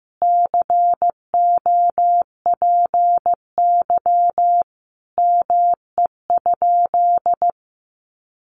Morse code, heard as a keyed tone, about 15 wpm, 700 Hz. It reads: COPY ME?
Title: Sentences from Top 1400 Words 15wpm